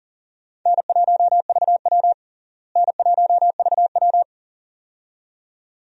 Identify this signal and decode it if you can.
Morse code, heard as a keyed tone, about 40 wpm, 700 Hz. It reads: N1VW N1VW
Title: USA Call Signs - 4 Characters - Repeated 2x - 40wpm